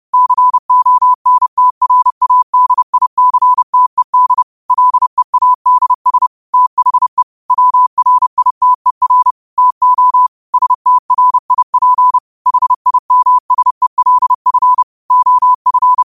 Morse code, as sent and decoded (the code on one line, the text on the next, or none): -.-. --- -. - .-. .- -.. .. -.-. - . -.. / .-.. .- -.. ... / - .... / .-- .-. .. - . .-. / - --- / ... - .-. .. .--. / .... .. -- ... .-.. ..-. / --- ..-.
CONTRADICTED LEADS THE WRITER TO STRIP HIMSELF OF